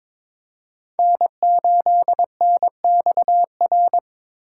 Morse code, heard as a keyed tone, about 22 wpm, 700 Hz. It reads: N8NXR
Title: USA Call Signs - 5 Characters - 6x Character Spacing - 22wpm